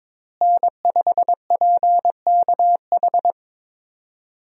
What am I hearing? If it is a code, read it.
Morse code, heard as a keyed tone, about 21 wpm, 700 Hz. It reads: N5PKH
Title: USA Call Signs - 5 Characters - Repeated 2x - 22wpm